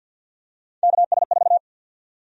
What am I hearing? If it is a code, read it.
Morse code, heard as a keyed tone, about 46 wpm, 700 Hz. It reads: KS4
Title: USA Call Sign Prefixes - 50wpm